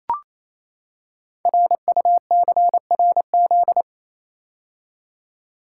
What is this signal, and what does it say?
Morse code, heard as a keyed tone, about 29 wpm, 700 Hz. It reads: RUCRZ
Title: Code Groups - 5 Letters - 28wpm